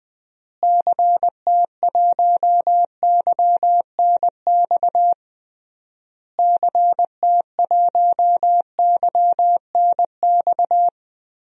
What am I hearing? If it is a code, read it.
Morse code, heard as a keyed tone, about 20 wpm, 700 Hz. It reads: CT1YNX CT1YNX
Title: DXCC-100 Call Signs - 6 Characters - Repeated 3x - 20wpm